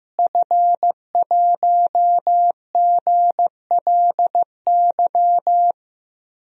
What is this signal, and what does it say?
Morse code, heard as a keyed tone, about 15 wpm, 700 Hz. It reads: F1GLY
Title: DXCC-100 Call Signs - 5 Characters - Repeated 2x - 15wpm